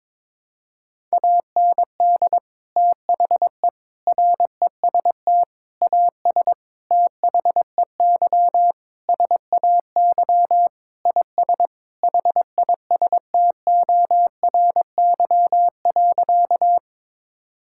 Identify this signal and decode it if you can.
Morse code, heard as a keyed tone, about 22 wpm, 700 Hz. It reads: AND THE REST AS THEY SAY IS HISTORY.
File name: practice / 22wpm / Sentences with 8-Word Phrases - 22wpm